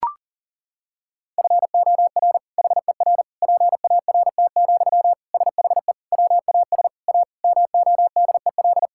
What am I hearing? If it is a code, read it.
Morse code, heard as a keyed tone, about 40 wpm, 700 Hz. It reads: FOR HER PART, SHE WAS A MODEL
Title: Sentences from Top 1000 Words - Review Part 5 - 40wpm